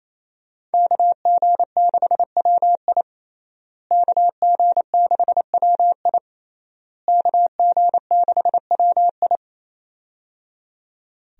Morse code, heard as a keyed tone, about 28 wpm, 700 Hz. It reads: KG6WI KG6WI KG6WI
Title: USA Call Signs - 5 Characters - Repeated 3x - 28wpm